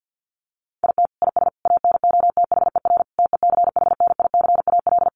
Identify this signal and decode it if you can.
Morse code, heard as a keyed tone, about 49 wpm, 700 Hz. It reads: IT IS ANOTHER TECHNICAL
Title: Sentences from Top 1100 Words 50wpm